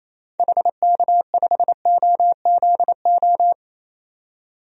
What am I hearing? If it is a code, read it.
Morse code, heard as a keyed tone, about 28 wpm, 700 Hz. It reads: HK5OZO
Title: DXCC-100 Call Signs - 6 Characters - 28wpm